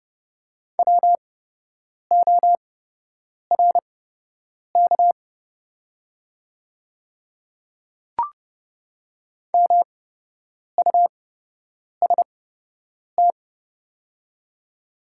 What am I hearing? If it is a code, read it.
Morse code, heard as a keyed tone, about 30 wpm, spaced as for 8 wpm, 700 Hz. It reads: WORK MUST
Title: 8x Character Spacing - Top 200 Words - 30wpm